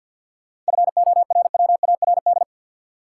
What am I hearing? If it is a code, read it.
Morse code, heard as a keyed tone, about 51 wpm, 700 Hz. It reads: FORWARD
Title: Sentences from Top 1300 Words - New Word Drill Alt - 50wpm